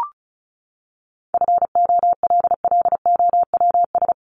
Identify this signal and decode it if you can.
Morse code, heard as a keyed tone, about 34 wpm, 700 Hz. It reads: FOLLOWS
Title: ES:S Suffix - Top 100 Words with Base Words 35wpm